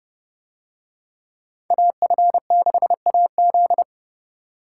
Morse code, heard as a keyed tone, about 29 wpm, 700 Hz. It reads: AF6AZ